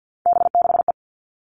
Morse code, heard as a keyed tone, about 50 wpm, 700 Hz. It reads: B6E
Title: All Cut Numbers - 3-Digit 50wpm